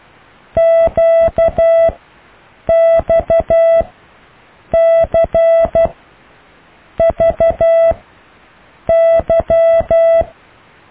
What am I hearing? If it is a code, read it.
Morse code, heard as a keyed tone, about 12 wpm, spaced as for 6 wpm, 660 Hz. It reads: QXCVY
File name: cw / Alle letters